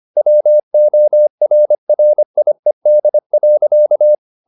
Morse code, heard as a keyed tone, about 25 wpm, 600 Hz. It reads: WORRIED.